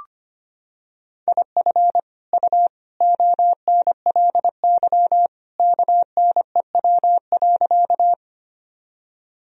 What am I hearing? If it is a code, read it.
Morse code, heard as a keyed tone, about 25 wpm, 700 Hz. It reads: IF U ONLY KNEW.
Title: Sentences from Top 400 Words and Abbreviations 25wpm